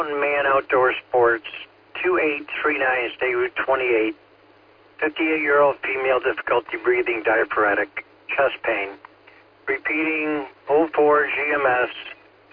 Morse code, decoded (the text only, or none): none